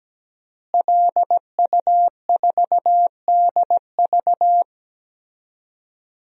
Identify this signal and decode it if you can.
Morse code, heard as a keyed tone, about 17 wpm, 700 Hz. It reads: LU4DV
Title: DXCC-100 Call Signs - 5 Characters - Repeated 2x - 17wpm